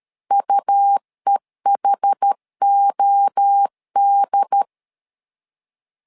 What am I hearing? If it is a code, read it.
Morse code, heard as a keyed tone, about 13 wpm, 800 Hz. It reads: UEHOD
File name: LEZIONI / 4 Q1S 12wpm 1802